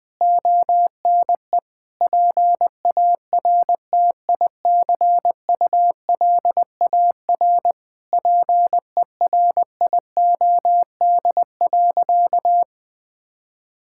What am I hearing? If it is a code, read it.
Morse code, heard as a keyed tone, about 21 wpm, 700 Hz. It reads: ONE PARTICULAR PERIOD.
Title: Sentences from Top 1000 Words - Review Part 5 - 20wpm